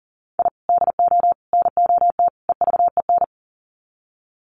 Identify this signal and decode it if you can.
Morse code, heard as a keyed tone, about 38 wpm, 700 Hz. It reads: I DO NOT EVEN
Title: Sets of 4 Words 40wpm